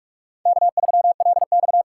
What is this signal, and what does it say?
Morse code, heard as a keyed tone, about 43 wpm, 700 Hz. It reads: K3LX
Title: USA Call Signs - 4 Characters - 3x Character Spacing - 45wpm